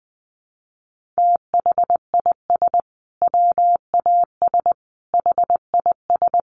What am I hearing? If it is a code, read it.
Morse code, heard as a keyed tone, about 20 wpm, 700 Hz. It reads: THIS WAS HIS
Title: Sentences from Top 1500 Words 20wpm